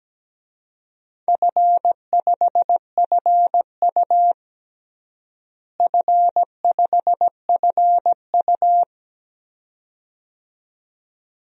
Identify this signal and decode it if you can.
Morse code, heard as a keyed tone, about 17 wpm, 700 Hz. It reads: F5FU F5FU
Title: DXCC-100 Call Signs - 4 Characters - Repeated 2x - 17wpm